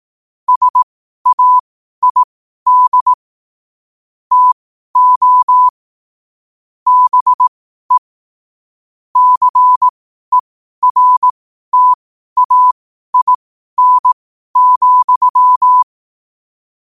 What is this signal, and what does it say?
Morse code, heard as a keyed tone, about 18 wpm, spaced as for 12 wpm, 1000 Hz. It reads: SAID TO BE CERTAIN,
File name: qotdfiles / 12wpm